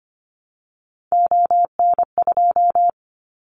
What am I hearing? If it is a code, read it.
Morse code, heard as a keyed tone, about 25 wpm, 700 Hz. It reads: ON2
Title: Callsign Prefixes for Top 50 DXCC - 25wpm